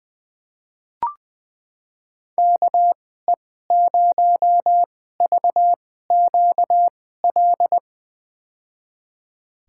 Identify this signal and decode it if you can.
Morse code, heard as a keyed tone, about 20 wpm, spaced as for 14 wpm, 700 Hz. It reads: KE0VQL
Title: USA Call Signs - 6 Characters - 2x Character Spacing - 20wpm